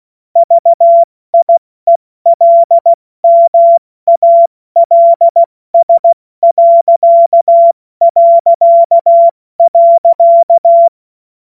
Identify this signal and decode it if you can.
Morse code, heard as a keyed tone, about 16 wpm, spaced as for 14 wpm, 680 Hz. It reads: VIELMALS...
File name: Morsekurs / Witze / Witz 06. Dezember 2022 70 F80